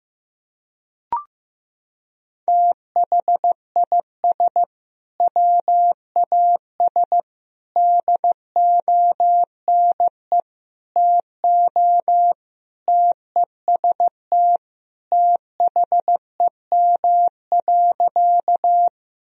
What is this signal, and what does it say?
Morse code, heard as a keyed tone, about 15 wpm, 700 Hz. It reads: THIS WAS DONE TO TEST THEM.
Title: Sentences from Top 1000 Words - Review Part 3 - 15wpm